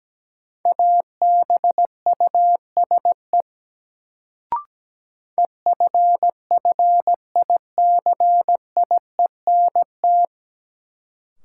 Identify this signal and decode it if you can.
Morse code, heard as a keyed tone, about 17 wpm, 700 Hz. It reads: ABUSE EFFICIENT
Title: Sentences from Top 1300 Words - New Word Drill Alt - 17wpm